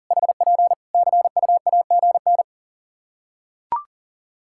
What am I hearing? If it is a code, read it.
Morse code, heard as a keyed tone, about 40 wpm, 700 Hz. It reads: HP CUAGN
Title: Practice QSO - Exchange 4 - Ending - 40wpm